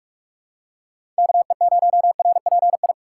Morse code, heard as a keyed tone, about 46 wpm, 700 Hz. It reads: KE0RPI
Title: USA Call Signs - 6 Characters - 2x Character Spacing - 45wpm